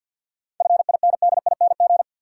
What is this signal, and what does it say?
Morse code, heard as a keyed tone, about 52 wpm, 700 Hz. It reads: FINDING
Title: Sentences from Top 1100 Words - New Word Drill Alt - 50wpm